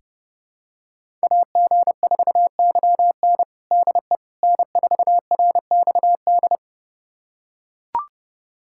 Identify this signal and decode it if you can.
Morse code, heard as a keyed tone, about 30 wpm, 700 Hz. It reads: AG4YN DE N4RXD
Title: USA Callsign DE Callsign - 5 Characters - 30wpm